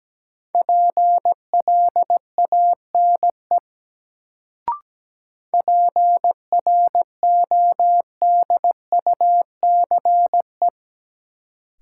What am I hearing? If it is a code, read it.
Morse code, heard as a keyed tone, about 17 wpm, 700 Hz. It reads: PLANE PRODUCE
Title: Sentences from Top 1000 Words - Review Part 3 - New Word Drill - 17wpm